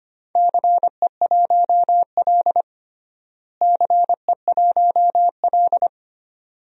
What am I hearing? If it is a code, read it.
Morse code, heard as a keyed tone, about 25 wpm, 700 Hz. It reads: CE1L CE1L
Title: DXCC-100 Call Signs - 4 Characters - Repeated 2x - 25wpm